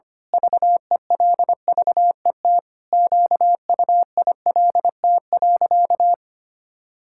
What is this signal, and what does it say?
Morse code, heard as a keyed tone, about 25 wpm, 700 Hz. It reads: VELVET QUILT.
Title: Pangram Challenge v2 - 25wpm